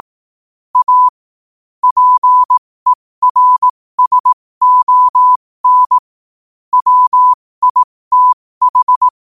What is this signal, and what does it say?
Morse code, heard as a keyed tone, about 18 wpm, spaced as for 15 wpm, 1000 Hz. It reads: A PERSON WITH
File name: qotdfiles / 15wpm